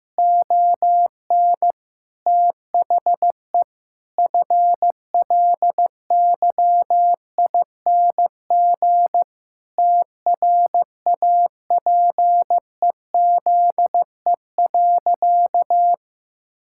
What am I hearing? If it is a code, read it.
Morse code, heard as a keyed tone, about 14 wpm, 700 Hz. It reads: ON THE FLYING TRAPEZE.
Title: Pangram Challenge v2 - 15wpm